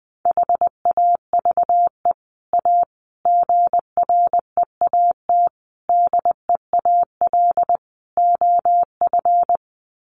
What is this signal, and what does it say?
Morse code, heard as a keyed tone, about 19 wpm, 700 Hz. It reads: HAVE A GREAT DEAL OF
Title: Sets of 5 Words Encore2 20wpm